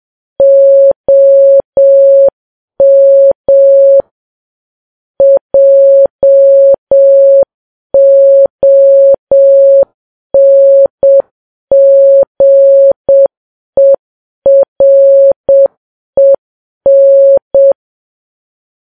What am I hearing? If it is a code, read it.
Morse code, heard as a keyed tone, about 7 wpm, 550 Hz. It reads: OM JONGEREN
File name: Files Web tls / Morse on9fv / morse7kps